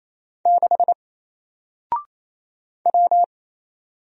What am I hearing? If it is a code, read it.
Morse code, heard as a keyed tone, about 27 wpm, 700 Hz. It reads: 6 W
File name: Single-Letter-Number-28wpm